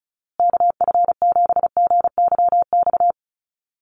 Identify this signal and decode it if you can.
Morse code, heard as a keyed tone, about 34 wpm, 700 Hz. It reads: KF7GYX
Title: USA Call Signs Repeated 3x 35wpm